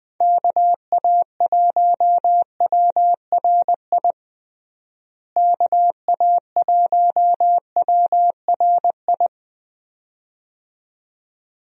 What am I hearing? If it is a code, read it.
Morse code, heard as a keyed tone, about 21 wpm, 700 Hz. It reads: KA1WRI KA1WRI
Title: USA Call Signs - 6 Characters - Repeated 2x - 20wpm